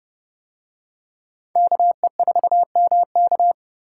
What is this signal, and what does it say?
Morse code, heard as a keyed tone, about 30 wpm, 700 Hz. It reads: KE4MK